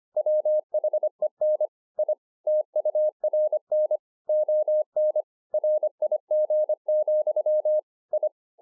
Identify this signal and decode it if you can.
Morse code, heard as a keyed tone, about 24 wpm, 610 Hz. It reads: WHEN I TURN ON RIG, I
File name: 2011-04-mp3 / BEGINNERS DIARY 25WPM